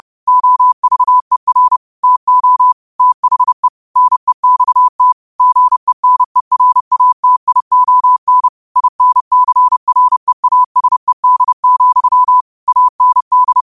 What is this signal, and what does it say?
Morse code, heard as a keyed tone, about 30 wpm, 1000 Hz. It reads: OVER TO THE NEXT GENERATION INCREASED, AND